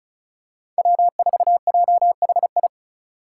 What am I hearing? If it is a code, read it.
Morse code, heard as a keyed tone, about 36 wpm, 700 Hz. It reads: W4JHI
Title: Five Challenging Letter-Number Combinations 35wpm